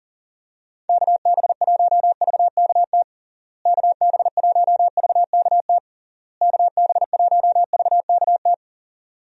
Morse code, heard as a keyed tone, about 37 wpm, 700 Hz. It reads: KB1VKT KB1VKT KB1VKT
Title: USA Call Signs - 6 Characters - Repeated 3x - 40wpm